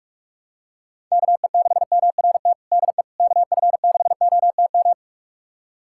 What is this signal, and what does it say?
Morse code, heard as a keyed tone, about 46 wpm, 700 Hz. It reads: KE6MRT DE KF6OTM
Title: USA Callsign DE Callsign - 6 Characters - 45wpm